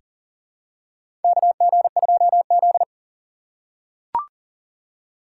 Morse code, heard as a keyed tone, about 40 wpm, 700 Hz. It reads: KG2Z
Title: USA Call Signs - 4 Characters - 4x Character Spacing - 40wpm